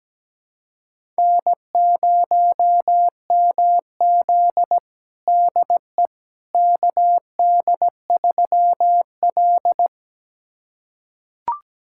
Morse code, heard as a keyed tone, about 17 wpm, 700 Hz. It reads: N0MZ DE KD3L
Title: USA Callsign DE Callsign - 4 Characters - 17wpm